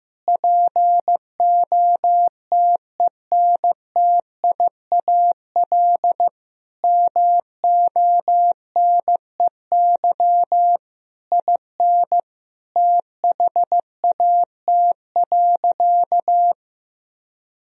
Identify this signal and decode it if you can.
Morse code, heard as a keyed tone, about 15 wpm, 700 Hz. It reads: POTENTIAL MONEY IN THAT.
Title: Sentences from Top 1000 Words - Review Part 4 - 15wpm